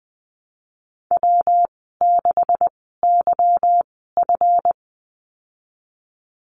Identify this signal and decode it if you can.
Morse code, heard as a keyed tone, about 20 wpm, spaced as for 15 wpm, 700 Hz. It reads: W6YF
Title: 179 Lesson 29g - Slash - Call Signs 20wpm